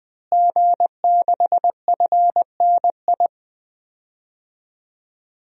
Morse code, heard as a keyed tone, about 20 wpm, 700 Hz. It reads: G6FNI
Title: DXCC-100 Call Signs - 5 Characters - Repeated 3x - 20wpm